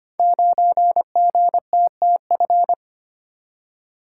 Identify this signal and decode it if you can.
Morse code, heard as a keyed tone, about 25 wpm, 700 Hz. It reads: 9GTTF
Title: Code Groups - 5 Letter:Numbers - 25wpm